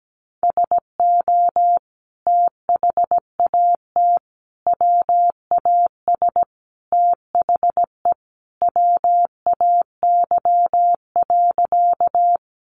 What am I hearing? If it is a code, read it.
Morse code, heard as a keyed tone, about 17 wpm, 700 Hz. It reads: SO THAT WAS THE WAY.
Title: Sentences from Top 100 Words 17wpm